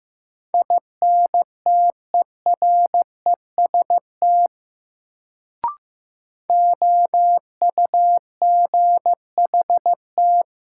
Morse code, heard as a keyed tone, about 15 wpm, 700 Hz. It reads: INTEREST OUGHT
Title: Sets of 5 Words Encore - New Word Drill - 15wpm